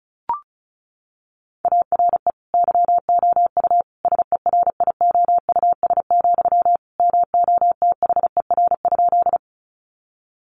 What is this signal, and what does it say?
Morse code, heard as a keyed tone, about 34 wpm, 700 Hz. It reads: ARE YOU SERIOUS, MOTHER?
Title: Sentences from Top 1100 Words 35wpm